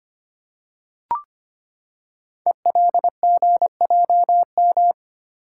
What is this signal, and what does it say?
Morse code, heard as a keyed tone, about 25 wpm, 700 Hz. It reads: ELGJM
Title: Code Groups - 5 Letters - 25wpm